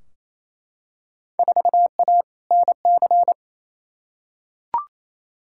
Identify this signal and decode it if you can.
Morse code, heard as a keyed tone, about 28 wpm, 700 Hz. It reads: 4A NC
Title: Field Day Exchanges Encore - 28wpm